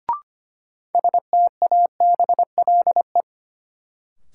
Morse code, heard as a keyed tone, about 25 wpm, 700 Hz. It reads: STABLE
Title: Sentences from Top 1500 Words - New Word Drill Alt - 25wpm